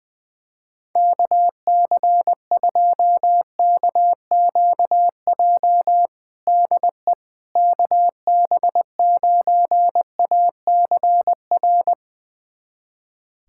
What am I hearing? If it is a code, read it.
Morse code, heard as a keyed tone, about 20 wpm, 700 Hz. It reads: KC2KQJ DE KB9ACR
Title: USA Callsign DE Callsign - 6 Characters - 20wpm